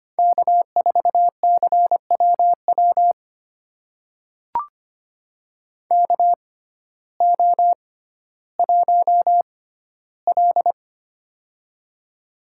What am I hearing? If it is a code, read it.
Morse code, heard as a keyed tone, about 25 wpm, spaced as for 14 wpm, 700 Hz. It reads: K4CWW KO1L
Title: USA Call Signs - 6x Character Spacing - 25wpm